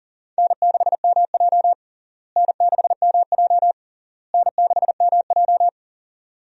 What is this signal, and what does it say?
Morse code, heard as a keyed tone, about 40 wpm, 700 Hz. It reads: N6MJ N6MJ N6MJ
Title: USA Call Signs - 4 Characters - Repeated 3x - 40wpm